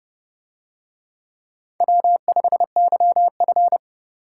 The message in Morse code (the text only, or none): W5YF